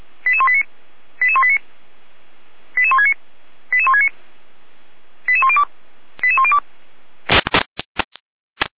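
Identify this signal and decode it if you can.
Five-tone selective call (ZVEI), heard as quick runs of five tones, tones of about 70 ms. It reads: 88189 88189 88169 88169 88102 88102
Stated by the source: ZVEI five-tone call received with an SDR.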